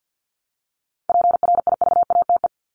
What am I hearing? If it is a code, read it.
Morse code, heard as a keyed tone, about 52 wpm, 700 Hz. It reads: PRIVATE
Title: Top 751-1000 Words 50wpm